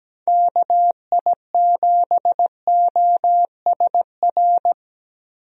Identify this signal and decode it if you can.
Morse code, heard as a keyed tone, about 17 wpm, 700 Hz. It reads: KI7OSR
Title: USA Call Signs - 6 Characters - 17wpm